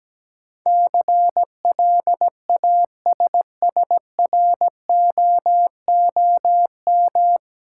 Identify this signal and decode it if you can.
Morse code, heard as a keyed tone, about 17 wpm, 700 Hz. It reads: CLASSROOM
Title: Sentences from Top 1500 Words - New Word Drill Alt - 17wpm